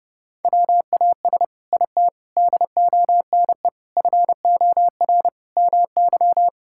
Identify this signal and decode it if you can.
Morse code, heard as a keyed tone, about 32 wpm, 700 Hz. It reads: WAS IT DONE FOR MY
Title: Sentences from Top 1000 Words - Review Part 6 - 30wpm